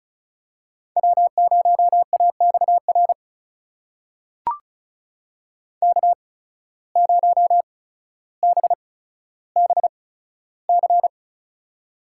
Morse code, heard as a keyed tone, about 36 wpm, spaced as for 15 wpm, 700 Hz. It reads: W0AXR K0BBC
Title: USA Call Signs - 5 Characters - 8x Character Spacing - 35wpm